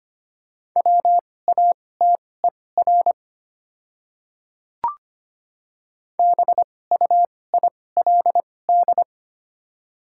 Morse code, heard as a keyed tone, about 25 wpm, spaced as for 18 wpm, 690 Hz. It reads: WATER BUILD
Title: 2x Character Spacing - Top 200 Words - 25wpm